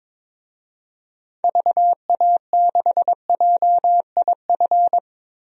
Morse code, heard as a keyed tone, about 22 wpm, 700 Hz. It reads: VA6JIF
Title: DXCC-100 Call Signs - 6 Characters - Repeated 3x - 22wpm